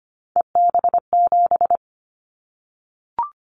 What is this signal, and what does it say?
Morse code, heard as a keyed tone, about 25 wpm, 700 Hz. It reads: EB7
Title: Callsign Prefixes for Top 25 DXCC - Repeated 3x - 25wpm